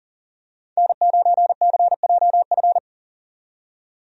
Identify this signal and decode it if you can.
Morse code, heard as a keyed tone, about 39 wpm, 700 Hz. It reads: N9CJF
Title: USA Call Signs - 5 Characters - 40wpm